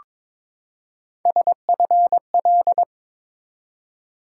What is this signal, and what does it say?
Morse code, heard as a keyed tone, about 22 wpm, 700 Hz. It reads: SFL
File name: ARRL Sections v2 - 22wpm